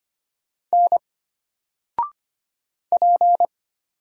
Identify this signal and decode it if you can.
Morse code, heard as a keyed tone, about 25 wpm, 700 Hz. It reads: N P
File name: Single-Letter-Number-25wpm